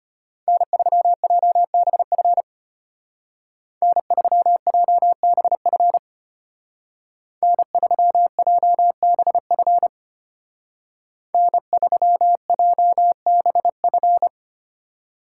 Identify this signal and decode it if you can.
Morse code, heard as a keyed tone, about 33 wpm, 700 Hz. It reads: N3JBF N3JBF N3JBF N3JBF